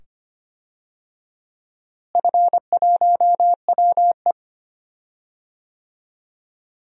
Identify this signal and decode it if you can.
Morse code, heard as a keyed tone, about 25 wpm, 700 Hz. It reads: F1WE
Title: DXCC-100 Call Signs - 4 Characters - Repeated 2x - 25wpm